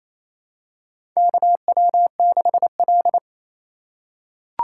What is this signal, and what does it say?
Morse code, heard as a keyed tone, about 28 wpm, 700 Hz. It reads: KW6L
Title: USA Call Signs - 4 Characters - 3x Character Spacing - 28wpm